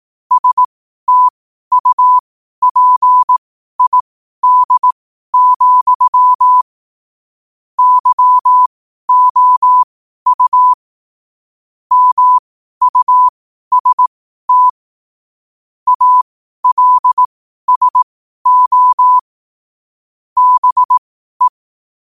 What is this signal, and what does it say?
Morse code, heard as a keyed tone, about 18 wpm, spaced as for 12 wpm, 1000 Hz. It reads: STUPID, YOU MUST ALSO BE
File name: qotdfiles / 12wpm